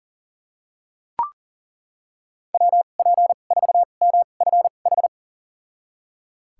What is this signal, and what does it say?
Morse code, heard as a keyed tone, about 39 wpm, spaced as for 28 wpm, 700 Hz. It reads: WP4MFH
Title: USA Call Signs - 6 Characters - 2x Character Spacing - 40wpm